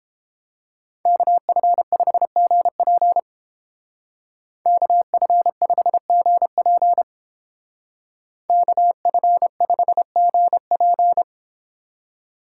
Morse code, heard as a keyed tone, about 29 wpm, 700 Hz. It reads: KF5GP KF5GP KF5GP